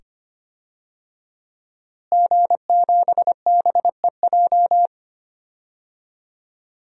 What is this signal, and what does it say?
Morse code, heard as a keyed tone, about 25 wpm, 700 Hz. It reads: G7BEJ
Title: DXCC-100 Call Signs - 5 Characters - Repeated 2x - 25wpm